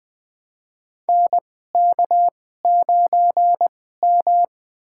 Morse code, heard as a keyed tone, about 20 wpm, spaced as for 15 wpm, 700 Hz. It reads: NK9M
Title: USA Call Signs - 4 Characters - 2x Character Spacing - 20wpm